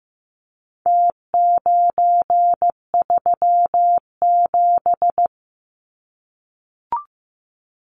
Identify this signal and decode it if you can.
Morse code, heard as a keyed tone, about 15 wpm, 700 Hz. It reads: T937
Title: Numbers - Contest Style - 4-Digit 15wpm